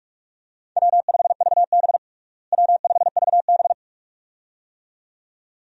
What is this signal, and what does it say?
Morse code, heard as a keyed tone, about 47 wpm, 700 Hz. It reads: W5VB W5VB
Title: USA Call Signs - 4 Characters - Repeated 2x - 45wpm